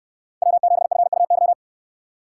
Morse code, heard as a keyed tone, about 66 wpm, 700 Hz. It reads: W7FUY